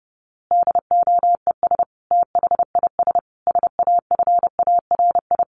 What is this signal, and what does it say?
Morse code, heard as a keyed tone, about 30 wpm, 700 Hz. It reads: DOES THIS SAFARI
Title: A Sound Of Thunder 30wpm